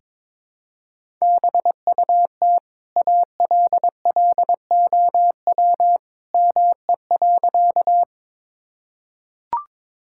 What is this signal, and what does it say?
Morse code, heard as a keyed tone, about 22 wpm, 700 Hz. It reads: BUT ALLOW ME.
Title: Sentences from Top 1000 Words - Review Part 5 - 22wpm